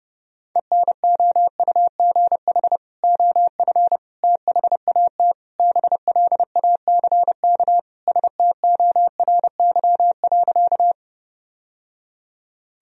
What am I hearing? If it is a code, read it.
Morse code, heard as a keyed tone, about 30 wpm, 700 Hz. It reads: ENOUGH OF THAT BLACK STORY.
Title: Sentences from Top 1000 Words - Review Part 2 - 30wpm